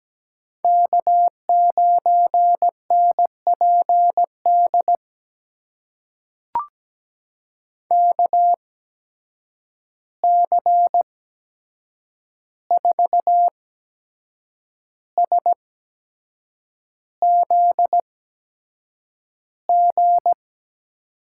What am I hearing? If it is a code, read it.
Morse code, heard as a keyed tone, about 17 wpm, spaced as for 7 wpm, 700 Hz. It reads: K9NPD KC4SZG
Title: USA Call Signs - 8x Character Spacing - 17wpm